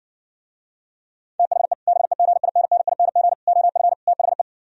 Morse code, heard as a keyed tone, about 69 wpm, 700 Hz. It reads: THE BEGINNING OF THE